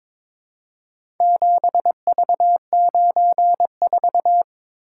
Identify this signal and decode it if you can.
Morse code, heard as a keyed tone, about 21 wpm, 700 Hz. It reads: 7V94
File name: practice / 22wpm / Four Letter-Number Combinations - 22wpm